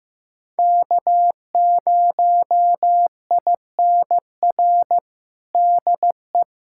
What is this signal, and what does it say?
Morse code, heard as a keyed tone, about 15 wpm, 700 Hz. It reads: K0INR DE
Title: USA Callsign DE Callsign - 5 Characters - 15wpm